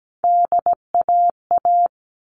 Morse code, heard as a keyed tone, about 17 wpm, 700 Hz. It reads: DAA
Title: All Cut Numbers - 3-Digit 17wpm